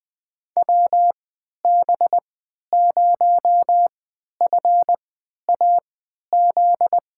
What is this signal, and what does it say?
Morse code, heard as a keyed tone, about 20 wpm, spaced as for 11 wpm, 700 Hz. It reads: WB0FAZ
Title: USA Call Signs - 3x Character Spacing - 20wpm